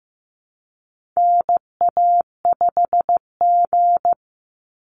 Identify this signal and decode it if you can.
Morse code, heard as a keyed tone, about 15 wpm, 700 Hz. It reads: NA5G
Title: Call Signs Repeated 3x - 15wpm